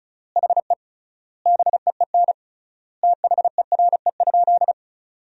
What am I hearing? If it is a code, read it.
Morse code, heard as a keyed tone, about 36 wpm, 700 Hz. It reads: HE BEEN THERE?
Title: 3x Word-Spacing - Sentences from Top 100 Words - 35wpm